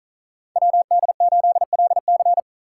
Morse code, heard as a keyed tone, about 47 wpm, 700 Hz. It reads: WD8LC